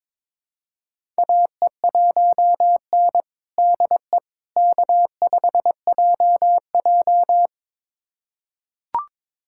Morse code, heard as a keyed tone, about 22 wpm, 700 Hz. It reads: AE1N DE K5JJ
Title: USA Callsign DE Callsign - 4 Characters - 22wpm